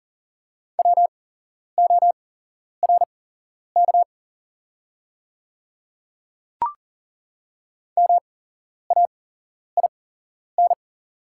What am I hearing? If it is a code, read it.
Morse code, heard as a keyed tone, about 40 wpm, spaced as for 11 wpm, 700 Hz. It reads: WORK MAIN
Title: 8x Character Spacing - Top 500 Words - 40wpm